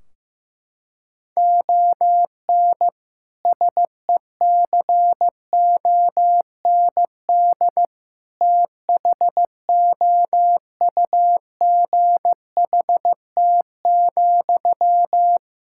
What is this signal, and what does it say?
Morse code, heard as a keyed tone, about 15 wpm, 700 Hz. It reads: ON SECOND THOUGHT,
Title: Sentences from Top 1600 Words 15wpm